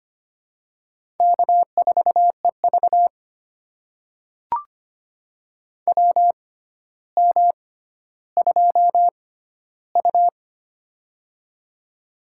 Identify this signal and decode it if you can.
Morse code, heard as a keyed tone, about 25 wpm, spaced as for 13 wpm, 700 Hz. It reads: K4EV WM2U
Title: USA Call Signs - 4 Characters - 6x Character Spacing - 25wpm